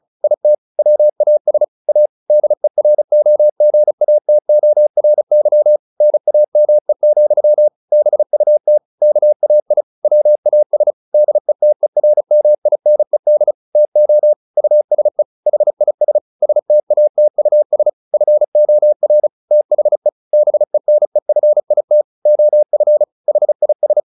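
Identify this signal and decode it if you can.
Morse code, heard as a keyed tone, about 36 wpm, 600 Hz. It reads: IT WAS A DEROGATORY NAME, BUT KAI WAS DETERMINED TO USE HIS STATUS FOR THE BENEFIT OF HIS
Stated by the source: Morse practice at 35 wpm from CW med Gnister, sending Ken Follett's novel Never in English.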